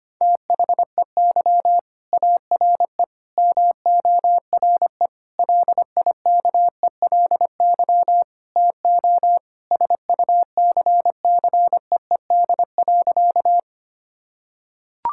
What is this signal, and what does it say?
Morse code, heard as a keyed tone, about 25 wpm, 700 Hz. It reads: THEY ARE MORE LIKELY TO SUCCEED.
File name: practice / 25wpm / Sentences with 4-Word Phrases - 25wpm